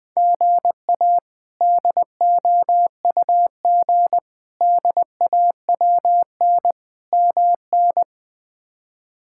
Morse code, heard as a keyed tone, about 21 wpm, 700 Hz. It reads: GA DOUG DAWN MN
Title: K1USN SST QSO Practice - 20wpm